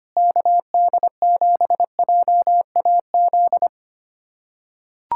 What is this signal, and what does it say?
Morse code, heard as a keyed tone, about 25 wpm, 700 Hz. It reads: KD7JAZ
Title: USA Callsign DE Callsign - 6 Characters - 25wpm